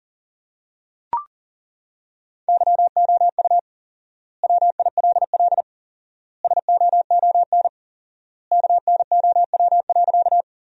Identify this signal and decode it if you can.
Morse code, heard as a keyed tone, about 37 wpm, 700 Hz. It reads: YOU WILL SOON KNOW.
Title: 4x Word-Spacing - Sentences from Top 400 Words - 40wpm